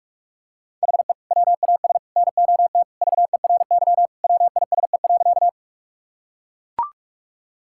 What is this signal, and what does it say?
Morse code, heard as a keyed tone, about 41 wpm, 700 Hz. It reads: HE WAS NOT VERY WISE.
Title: Sentences from Top 1600 Words 45wpm